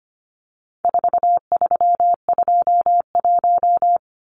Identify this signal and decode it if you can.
Morse code, heard as a keyed tone, about 25 wpm, 700 Hz. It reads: 4321
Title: Numbers - 4-Digit 25wpm